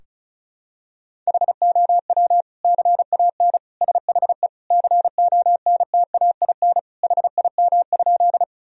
Morse code, heard as a keyed tone, about 34 wpm, 700 Hz. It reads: HOW CAN SHE CONTAIN HIM?
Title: Sentences from Top 1000 Words - Review Part 2 - 35wpm